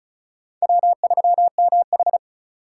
Morse code, heard as a keyed tone, about 36 wpm, 700 Hz. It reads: W3MH
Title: USA Call Signs - 4 Characters - 2x Character Spacing - 35wpm